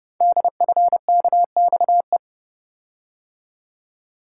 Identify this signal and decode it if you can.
Morse code, heard as a keyed tone, about 30 wpm, 700 Hz. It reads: DFKXE
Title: Code Groups - 5 Letters - 30wpm